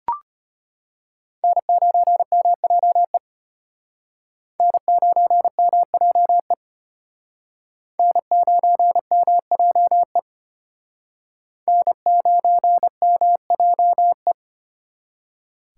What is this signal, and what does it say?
Morse code, heard as a keyed tone, about 31 wpm, 700 Hz. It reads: N9MJE N9MJE N9MJE N9MJE
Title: Speed Racing - USA 1x3 Call Signs 25wpm